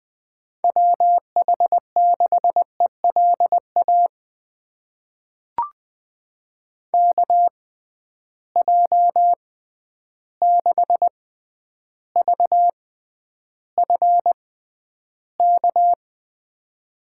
Morse code, heard as a keyed tone, about 20 wpm, spaced as for 10 wpm, 700 Hz. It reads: WH6ELA KJ6VFK
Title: USA Call Signs - 6 Characters - 6x Character Spacing - 20wpm